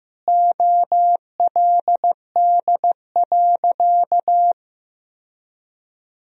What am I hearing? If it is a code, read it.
Morse code, heard as a keyed tone, about 15 wpm, 700 Hz. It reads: OLD.
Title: Sentences from Top 1000 Words - Review Part 2 - 15wpm